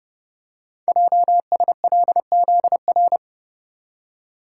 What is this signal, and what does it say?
Morse code, heard as a keyed tone, about 30 wpm, 700 Hz. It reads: JSLZR
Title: Code Groups - 5 Letters - 30wpm